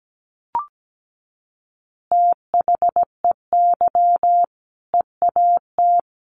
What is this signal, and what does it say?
Morse code, heard as a keyed tone, about 17 wpm, 700 Hz. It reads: THEY EAT
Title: Easy Phrases 17wpm